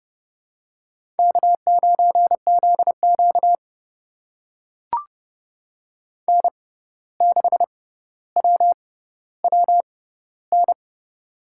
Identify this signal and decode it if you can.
Morse code, heard as a keyed tone, about 30 wpm, spaced as for 14 wpm, 700 Hz. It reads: K9ZQ N6WWN
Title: USA Call Signs - 6x Character Spacing - 30wpm